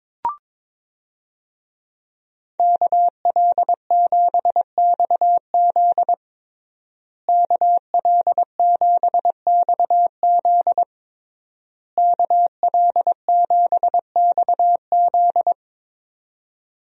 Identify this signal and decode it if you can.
Morse code, heard as a keyed tone, about 22 wpm, 700 Hz. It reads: KL7XZ KL7XZ KL7XZ
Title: DXCC-100 Call Signs - 5 Characters - Repeated 3x - 22wpm